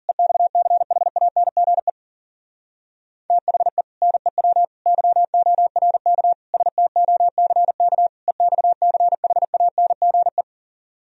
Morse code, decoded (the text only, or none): EXCHANGE THE NEW YORK STOCK EXCHANGE